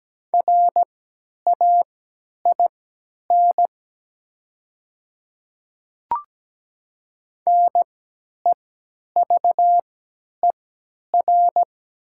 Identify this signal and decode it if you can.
Morse code, heard as a keyed tone, about 17 wpm, spaced as for 10 wpm, 700 Hz. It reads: RAIN NEVER
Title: 3x Character Spacing - Top 300 Words - 17wpm